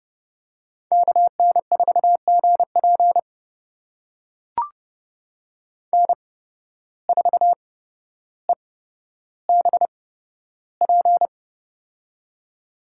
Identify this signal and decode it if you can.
Morse code, heard as a keyed tone, about 30 wpm, spaced as for 13 wpm, 700 Hz. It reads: KN4GP N4EBP
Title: USA Call Signs - 5 Characters - 8x Character Spacing - 30wpm